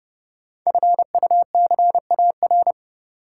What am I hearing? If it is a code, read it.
Morse code, heard as a keyed tone, about 30 wpm, 700 Hz. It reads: FUCAR
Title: Code Groups - 5 Letter:Numbers - 30wpm